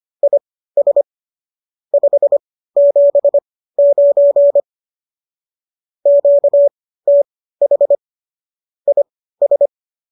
Morse code, heard as a keyed tone, about 25 wpm, spaced as for 15 wpm, 570 Hz. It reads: IS 579 QTH IS
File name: qso204 15